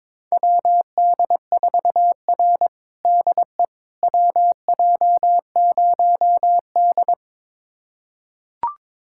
Morse code, heard as a keyed tone, about 22 wpm, 700 Hz. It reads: WD4R DE WJ0D
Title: USA Callsign DE Callsign - 4 Characters - 22wpm